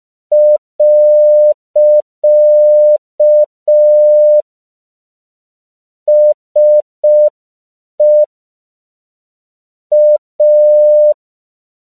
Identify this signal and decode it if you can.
Morse code, heard as a keyed tone, about 5 wpm, 600 Hz. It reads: . SE A
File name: 5PPM